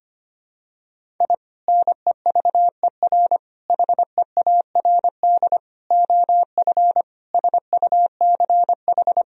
Morse code, heard as a keyed tone, about 25 wpm, 700 Hz. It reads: I NEVER HEARD OF SUCH
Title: Sentences from Top 1000 Words - Review Part 3 - 25wpm